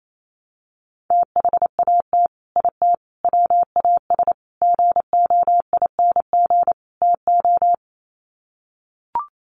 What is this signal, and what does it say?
Morse code, heard as a keyed tone, about 28 wpm, 700 Hz. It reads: THAT IT WAS GOING TO
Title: Sets of 5 Words Encore2 28wpm